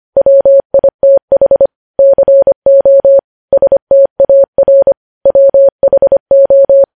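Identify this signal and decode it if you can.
Morse code, heard as a keyed tone, about 25 wpm, 550 Hz. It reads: WITH CO STAR WHO
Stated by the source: News headlines sent in Morse at 25 wpm.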